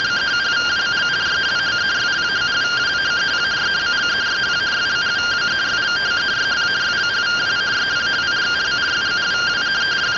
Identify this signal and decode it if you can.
Two-tone teleprinter signal, about 170 Hz shift.